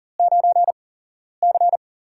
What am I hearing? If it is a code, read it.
Morse code, heard as a keyed tone, about 40 wpm, spaced as for 11 wpm, 700 Hz. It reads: 9C
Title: USA Call Signs - 8x Character Spacing - 40wpm